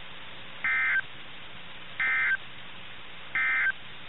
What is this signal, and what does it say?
Emergency Alert System (SAME) tones, heard as harsh buzzing bursts. It is heard from 0.5 to 3.8 s.
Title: NOAA Weather Radio - KIH61